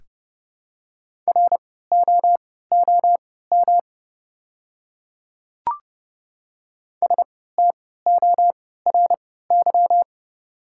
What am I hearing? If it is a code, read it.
Morse code, heard as a keyed tone, about 30 wpm, spaced as for 17 wpm, 700 Hz. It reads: ROOM STORY
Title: 3x Character Spacing - Top 500 Words - 30wpm